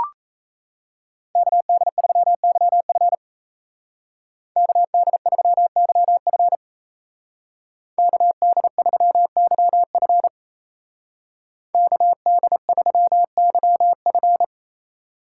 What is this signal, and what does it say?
Morse code, heard as a keyed tone, about 37 wpm, 700 Hz. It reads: KD3YF KD3YF KD3YF KD3YF